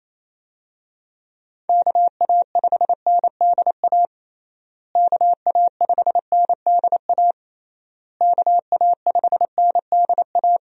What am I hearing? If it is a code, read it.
Morse code, heard as a keyed tone, about 28 wpm, 700 Hz. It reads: KA5NDA KA5NDA KA5NDA
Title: USA Call Signs - 6 Characters - Repeated 3x - 28wpm